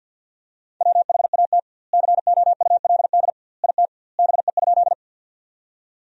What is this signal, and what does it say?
Morse code, heard as a keyed tone, about 50 wpm, 700 Hz. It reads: WHAT COULD IT BE?
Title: 2x Word-Spacing - Sentences from Top 100 Words - 50wpm